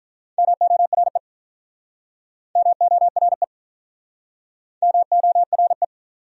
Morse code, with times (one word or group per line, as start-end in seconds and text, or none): 0.4-1.2 MORE
2.6-3.4 MORE
4.8-5.8 MORE